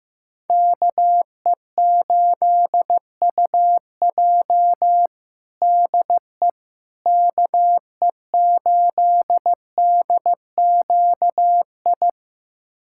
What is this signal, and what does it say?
Morse code, heard as a keyed tone, about 15 wpm, 700 Hz. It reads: KE8UJ DE KE8DQI
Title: USA Callsign DE Callsign 15wpm